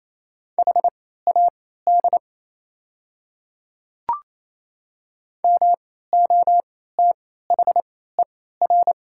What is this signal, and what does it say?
Morse code, heard as a keyed tone, about 28 wpm, spaced as for 16 wpm, 700 Hz. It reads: HAD MOTHER
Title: 3x Character Spacing - Top 200 Words - 28wpm